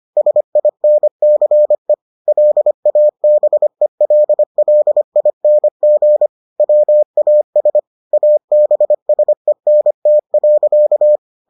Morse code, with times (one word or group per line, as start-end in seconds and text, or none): 0.2-1.9 SINCE
2.3-6.3 LABELLING
6.6-7.8 WAS
8.1-11.2 ABSENT.